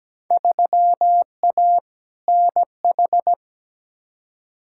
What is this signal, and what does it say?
Morse code, heard as a keyed tone, about 17 wpm, 700 Hz. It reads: 3A NH
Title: Field Day Exchanges - 17wpm